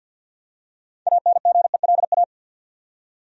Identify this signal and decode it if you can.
Morse code, heard as a keyed tone, about 52 wpm, 700 Hz. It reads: ANGELA
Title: USA Common Names - 50wpm